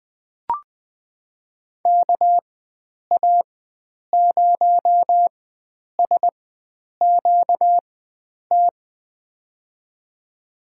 Morse code, heard as a keyed tone, about 20 wpm, spaced as for 9 wpm, 700 Hz. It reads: KA0SQT
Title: USA Call Signs - 6 Characters - 4x Character Spacing - 20wpm